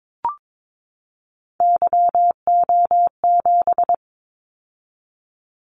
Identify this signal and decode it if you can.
Morse code, heard as a keyed tone, about 22 wpm, 700 Hz. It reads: YO7
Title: Callsign Prefixes for Top 25 DXCC - 22wpm